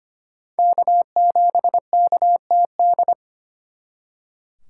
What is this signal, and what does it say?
Morse code, heard as a keyed tone, about 32 wpm, 700 Hz. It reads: K7KTD